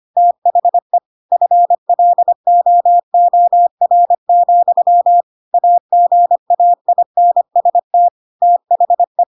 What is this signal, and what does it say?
Morse code, heard as a keyed tone, about 25 wpm, 700 Hz. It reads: THE FLOOR, AGAINST THE
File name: Pinocchio / Pinocchio ch1 25 25 ik1hge